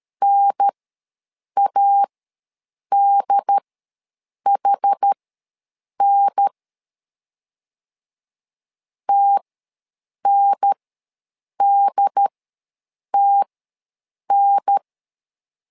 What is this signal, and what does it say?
Morse code, heard as a keyed tone, about 13 wpm, spaced as for 7 wpm, 800 Hz. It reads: NADHN TNDTN
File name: LEZIONI / 2 UHDN / 2 UHDN ws 12wpm 1708